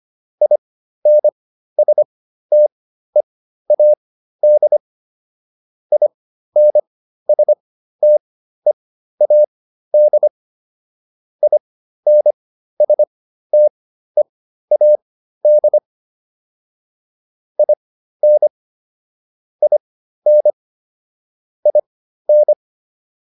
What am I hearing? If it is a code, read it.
Morse code, heard as a keyed tone, about 25 wpm, spaced as for 13 wpm, 600 Hz. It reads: INSTEAD INSTEAD INSTEAD IN IN IN